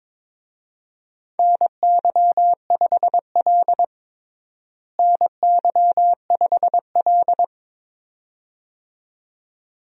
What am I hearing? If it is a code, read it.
Morse code, heard as a keyed tone, about 22 wpm, 700 Hz. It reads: NY5L NY5L
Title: USA Call Signs - 4 Characters - Repeated 2x - 22wpm